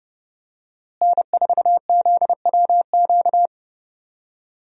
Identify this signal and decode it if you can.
Morse code, heard as a keyed tone, about 32 wpm, 700 Hz. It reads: N4ZWQ
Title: USA Call Signs - 5 Characters - Repeated 3x - 30wpm